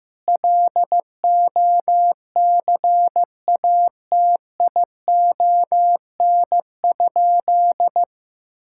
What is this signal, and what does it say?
Morse code, heard as a keyed tone, about 15 wpm, 700 Hz. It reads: LOCATION?
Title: Sentences from Top 1000 Words - Review Part 6 - 15wpm